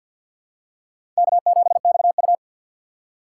Morse code, heard as a keyed tone, about 48 wpm, 700 Hz. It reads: K7XU
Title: USA Call Signs - 4 Characters - 6x Character Spacing - 50wpm